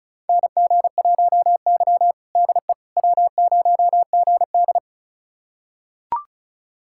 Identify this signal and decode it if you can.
Morse code, heard as a keyed tone, about 34 wpm, 700 Hz. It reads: NG1Y DE W0GD
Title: USA Callsign DE Callsign - 4 Characters - 35wpm